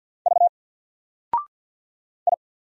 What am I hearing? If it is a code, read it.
Morse code, heard as a keyed tone, about 48 wpm, 700 Hz. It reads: V I
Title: Single Character (Letter:Number:Punctuation:Prosign) 50wpm